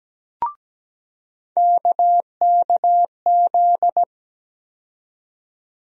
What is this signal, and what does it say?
Morse code, heard as a keyed tone, about 17 wpm, 700 Hz. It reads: KKZ
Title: USA Call Sign Postfixes - 3 Letters 17wpm